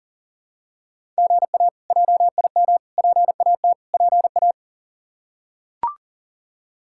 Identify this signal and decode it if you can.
Morse code, heard as a keyed tone, about 40 wpm, 700 Hz. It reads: GA JIM PAT PA
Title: K1USN SST QSO Practice - 40wpm